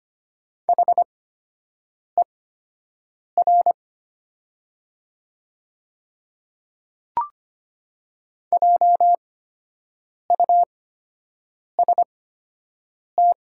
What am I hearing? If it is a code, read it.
Morse code, heard as a keyed tone, about 25 wpm, spaced as for 7 wpm, 700 Hz. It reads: HER JUST
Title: 8x Character Spacing - Top 200 Words - 25wpm